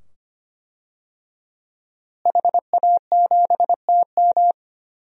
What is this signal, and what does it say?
Morse code, heard as a keyed tone, about 25 wpm, 700 Hz. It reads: HA7TM
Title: DXCC-100 Call Signs - 5 Characters - Repeated 2x - 25wpm